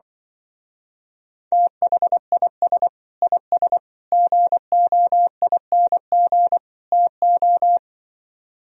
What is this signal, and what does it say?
Morse code, heard as a keyed tone, about 25 wpm, 700 Hz. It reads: THIS IS GOING TO
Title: Speed-Racing - Sets of 4 Words 20wpm